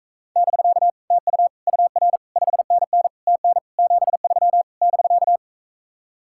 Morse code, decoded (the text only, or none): <BK> TU UR 5NN TN 73 <BK>